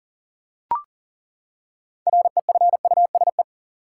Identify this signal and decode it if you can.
Morse code, heard as a keyed tone, about 40 wpm, 700 Hz. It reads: REFUSE
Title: Six Letter Words - 40wpm